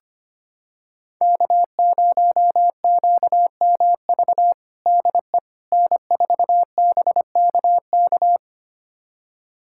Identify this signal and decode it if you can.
Morse code, heard as a keyed tone, about 25 wpm, 700 Hz. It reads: K0QMV DE N4BKK
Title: USA Callsign DE Callsign - 5 Characters - 25wpm